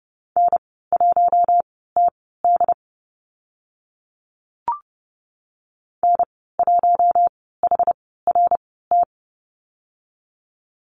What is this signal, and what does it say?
Morse code, heard as a keyed tone, about 32 wpm, spaced as for 17 wpm, 700 Hz. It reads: N1TD N1HRT
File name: Lesson 14g - Number 2 - Call Signs 30wpm